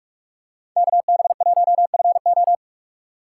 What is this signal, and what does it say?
Morse code, heard as a keyed tone, about 44 wpm, 700 Hz. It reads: KB1FO